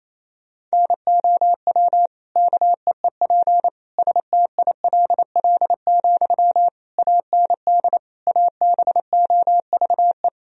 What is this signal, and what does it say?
Morse code, heard as a keyed tone, about 28 wpm, 700 Hz. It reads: NOW KEEP STILL, AND ABOVE
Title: Sentences from Top 1000 Words - Review Part 2 - 28wpm